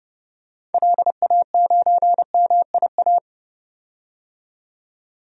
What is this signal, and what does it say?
Morse code, heard as a keyed tone, about 31 wpm, 690 Hz. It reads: LA9MIA
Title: DXCC-100 Call Signs - 6 Characters - 30wpm